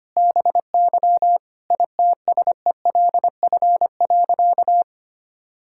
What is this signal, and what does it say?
Morse code, heard as a keyed tone, about 25 wpm, 700 Hz. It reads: BY ITSELF.
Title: Sentences from Top 1000 Words - Review Part 5 - 25wpm